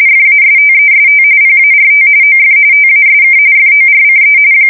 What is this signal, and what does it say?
RTTY, about 45.45 baud. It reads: FASTEST PART OF TASK FORCE
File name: fsk170 / USS-New-Jersey